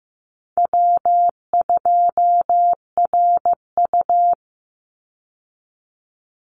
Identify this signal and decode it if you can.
Morse code, heard as a keyed tone, about 15 wpm, 700 Hz. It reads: W2RU
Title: Easy Call Signs Repeated 2x - 15wpm